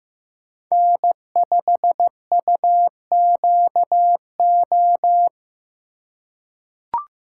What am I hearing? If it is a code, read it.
Morse code, heard as a keyed tone, about 15 wpm, 700 Hz. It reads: N5UQO
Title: USA Call Signs - 5 Characters - 4x Character Spacing - 15wpm